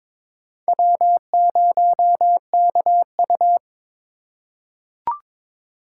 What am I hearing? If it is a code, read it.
Morse code, heard as a keyed tone, about 22 wpm, 700 Hz. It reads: W0KU
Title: USA Call Signs - 4 Characters - 3x Character Spacing - 22wpm